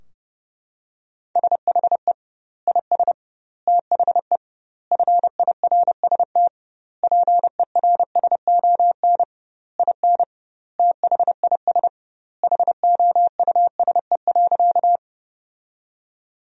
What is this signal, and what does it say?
Morse code, heard as a keyed tone, about 32 wpm, 700 Hz. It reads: SHE IS THE FIRST PERSON IN THIS HOUSE.
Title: 2x Word-Spacing - Sentences from Top 400 Words - 30wpm